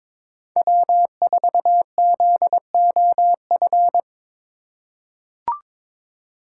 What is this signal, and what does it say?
Morse code, heard as a keyed tone, about 21 wpm, 700 Hz. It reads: W4ZOF
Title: Code Groups - 5 Letter:Numbers - 22wpm